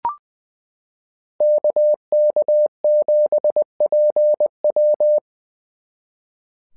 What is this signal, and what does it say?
Morse code, heard as a keyed tone, about 20 wpm, 600 Hz. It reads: KK7PW
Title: SOTA QSO Practice 20wpm